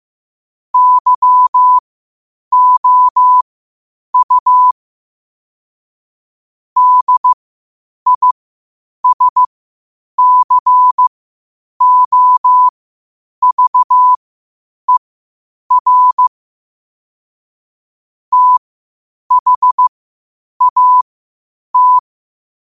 Morse code, heard as a keyed tone, about 15 wpm, spaced as for 8 wpm, 1000 Hz. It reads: YOU DISCOVER THAT
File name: qotdfiles / 8wpm